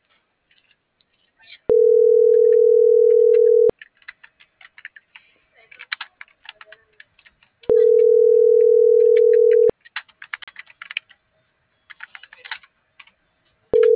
Ringback tone (North America), 440 + 480 Hz.
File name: CENTRAL MAYAB / out fernanda.pamela - .3869